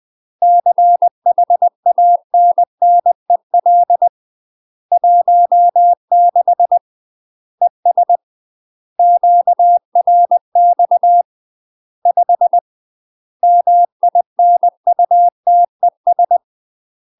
Morse code, heard as a keyed tone, about 20 wpm, 700 Hz. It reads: CHANNEL 16 ES QRX 5 MINUTES
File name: UK MASTER QSOs TRAINING / 20WPM / UK MASTER QSOs 0038